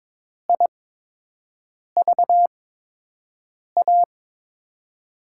Morse code, heard as a keyed tone, about 22 wpm, spaced as for 6 wpm, 700 Hz. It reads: IVA
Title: USA Call Signs - 6 Characters - 8x Character Spacing - 22wpm